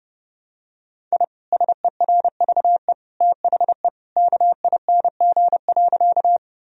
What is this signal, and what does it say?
Morse code, heard as a keyed tone, about 30 wpm, 700 Hz. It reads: I SERVE THE KING.
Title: Sentences from Top 1000 Words - Review Part 2 - 30wpm